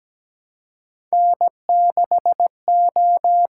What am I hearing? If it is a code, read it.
Morse code, heard as a keyed tone, about 17 wpm, 700 Hz. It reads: N6O
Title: USA Call Signs - 5 Characters - 6x Character Spacing - 17wpm